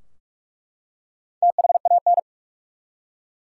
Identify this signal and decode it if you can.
Morse code, heard as a keyed tone, about 47 wpm, 700 Hz. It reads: THAN
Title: Four Letter Words - 45wpm